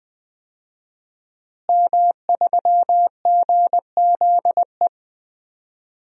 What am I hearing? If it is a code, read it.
Morse code, heard as a keyed tone, about 20 wpm, 700 Hz. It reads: M3GZE
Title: DXCC-100 Call Signs - 5 Characters - Repeated 2x - 20wpm